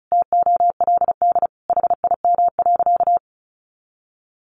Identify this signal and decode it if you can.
Morse code, heard as a keyed tone, about 34 wpm, 700 Hz. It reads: TOLD HIM.
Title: Sentences from Top 1200 Words 35wpm